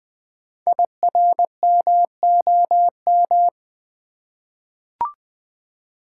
Morse code, heard as a keyed tone, about 20 wpm, 700 Hz. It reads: IRMOM
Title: Code Groups - 5 Letters - 20wpm